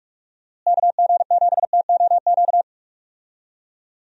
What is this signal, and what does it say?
Morse code, heard as a keyed tone, about 46 wpm, 700 Hz. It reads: KG7TOQ
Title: USA Call Signs - 6 Characters - Repeated 3x - 45wpm